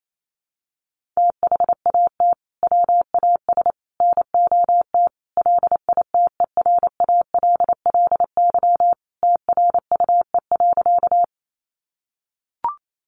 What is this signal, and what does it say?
Morse code, heard as a keyed tone, about 28 wpm, 700 Hz. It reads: THAT WAS NOT LITERALLY TRUE.
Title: Sentences from Top 1400 Words 28wpm